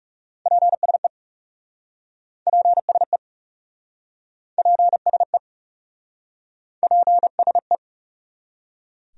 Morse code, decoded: PSE PSE PSE PSE

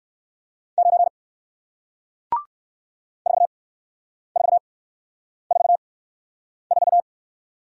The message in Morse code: 9 4 4 4 4

